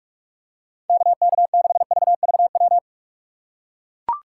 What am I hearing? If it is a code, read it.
Morse code, heard as a keyed tone, about 41 wpm, 700 Hz. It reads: KK6VVW
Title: USA Call Signs - 6 Characters - 6x Character Spacing - 45wpm